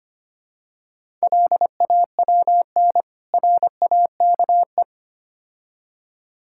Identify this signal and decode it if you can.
Morse code, heard as a keyed tone, about 29 wpm, 700 Hz. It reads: LAWN RAKE
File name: Easy Words - Speed-Racing with Hints - 25wpm